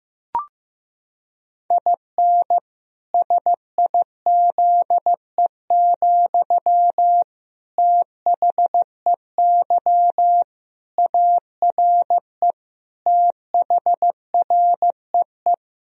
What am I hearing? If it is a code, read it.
Morse code, heard as a keyed tone, about 15 wpm, 700 Hz. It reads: IN SIZE, THEY ARE THREE
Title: Sentences from Top 1000 Words - Review Part 2 - 15wpm